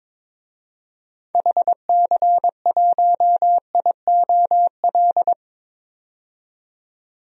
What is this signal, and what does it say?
Morse code, heard as a keyed tone, about 22 wpm, 700 Hz. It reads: HC1IOL
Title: DXCC-100 Call Signs - 6 Characters - 22wpm